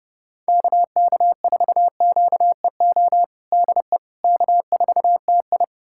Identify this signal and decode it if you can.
Morse code, heard as a keyed tone, about 30 wpm, 700 Hz. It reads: KK4QEO DE K4TI
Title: USA Callsign DE Callsign 30wpm